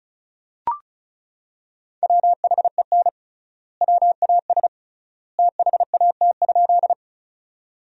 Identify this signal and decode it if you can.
Morse code, heard as a keyed tone, about 36 wpm, 700 Hz. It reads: WHEN WAS THAT?
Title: 3x Word-Spacing - Sentences from Top 100 Words - 35wpm